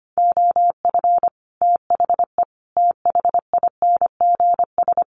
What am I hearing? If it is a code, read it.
Morse code, heard as a keyed tone, about 25 wpm, 700 Hz. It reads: OF THE THINGS
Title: Sets of 5 Words Encore 25wpm